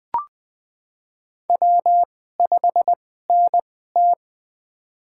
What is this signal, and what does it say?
Morse code, heard as a keyed tone, about 21 wpm, spaced as for 15 wpm, 700 Hz. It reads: W5NT
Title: USA Call Signs - 4 Characters - 2x Character Spacing - 20wpm